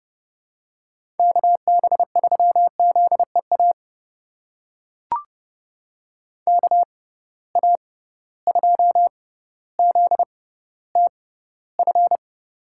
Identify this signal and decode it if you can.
Morse code, heard as a keyed tone, about 30 wpm, spaced as for 15 wpm, 700 Hz. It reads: KB3ZEA KA2ZTF
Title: USA Call Signs - 6 Characters - 6x Character Spacing - 30wpm